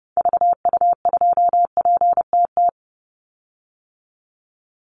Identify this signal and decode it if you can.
Morse code, heard as a keyed tone, about 30 wpm, 700 Hz. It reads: VU2PTT
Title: Call Signs Repeated 3x - 30wpm